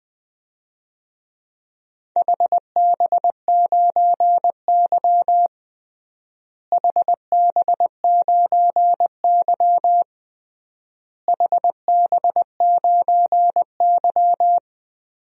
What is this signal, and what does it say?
Morse code, heard as a keyed tone, about 21 wpm, 700 Hz. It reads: HB9Y HB9Y HB9Y
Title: DXCC-100 Call Signs - 4 Characters - Repeated 3x - 20wpm